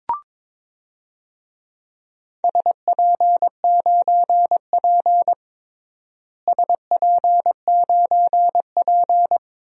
Morse code, heard as a keyed tone, about 22 wpm, 700 Hz. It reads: SP9P SP9P
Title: DXCC-100 Call Signs - 4 Characters - Repeated 2x - 22wpm